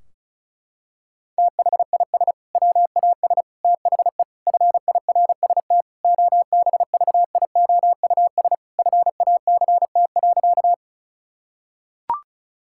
Morse code, heard as a keyed tone, about 36 wpm, 700 Hz. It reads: THIS WAS THE FIRST OBVIOUS FACT.
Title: Sentences from Top 1600 Words 35wpm